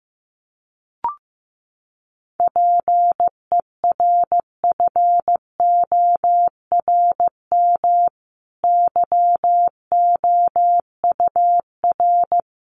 Morse code, heard as a keyed tone, about 15 wpm, 700 Hz. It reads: PERFORM YOUR
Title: Sentences from Top 1500 Words 15wpm